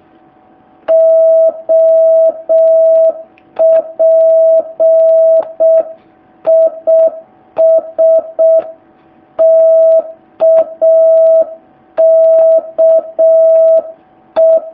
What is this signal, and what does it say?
Morse code, heard as a keyed tone, about 6 wpm, 650 Hz. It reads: OP IS TAKE
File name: jh8cbh / 2019cw